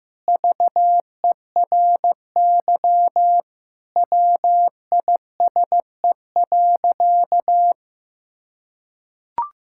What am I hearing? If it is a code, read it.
Morse code, heard as a keyed tone, about 15 wpm, 700 Hz. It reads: VERY WISE.
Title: Sentences from Top 1600 Words 15wpm